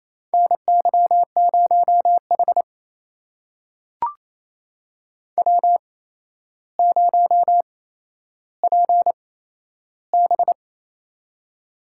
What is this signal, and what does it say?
Morse code, heard as a keyed tone, about 28 wpm, spaced as for 12 wpm, 700 Hz. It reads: NY0H W0PB
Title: USA Call Signs - 4 Characters - 8x Character Spacing - 28wpm